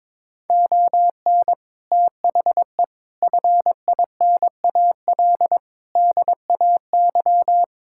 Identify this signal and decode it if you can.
Morse code, heard as a keyed tone, about 22 wpm, 700 Hz. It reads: ON THE FINAL DAY
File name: Sentences with 7-Word Phrases - 22wpm